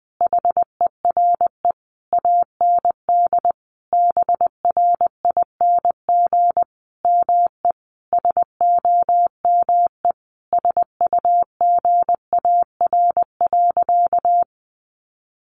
Morse code, heard as a keyed tone, about 19 wpm, 700 Hz. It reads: HERE AND BRING ME SOME SUGAR.
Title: Sentences from Top 1200 Words 20wpm